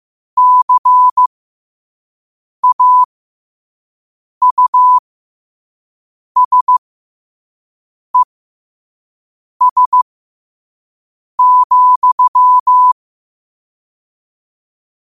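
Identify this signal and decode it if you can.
Morse code, heard as a keyed tone, about 15 wpm, spaced as for 5 wpm, 1000 Hz. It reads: CAUSES,
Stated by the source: Morse code practice at 5 wpm.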